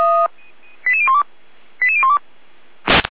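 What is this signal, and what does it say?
Five-tone selective call (ZVEI), heard as quick runs of five tones, tones of about 70 ms. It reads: 88012 88012
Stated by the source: ZVEI five-tone selective call 88102 received by SDR.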